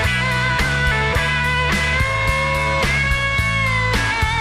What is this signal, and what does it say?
Emergency Alert System (SAME) tones, heard as harsh buzzing bursts. It is heard from 0.5 to 3.3 s.